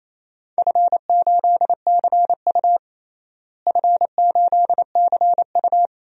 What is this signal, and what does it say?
Morse code, heard as a keyed tone, about 28 wpm, 700 Hz. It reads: F8CU F8CU
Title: DXCC-100 Call Signs - 4 Characters - Repeated 2x - 28wpm